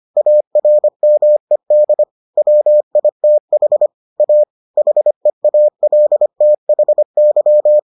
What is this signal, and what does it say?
Morse code, heard as a keyed tone, about 25 wpm, 600 Hz. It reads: ARMED WITH A HEALTHY